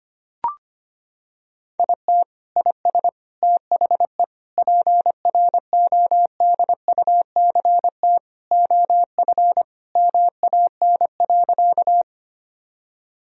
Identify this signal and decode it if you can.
Morse code, heard as a keyed tone, about 25 wpm, 700 Hz. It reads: IT IS THE PRODUCT OF MAN.
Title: Sentences from Top 1000 Words - Review Part 2 - 25wpm